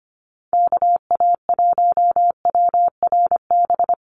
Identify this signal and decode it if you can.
Morse code, heard as a keyed tone, about 25 wpm, 700 Hz. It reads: KA1WRB
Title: USA Call Signs Repeated 3x 25wpm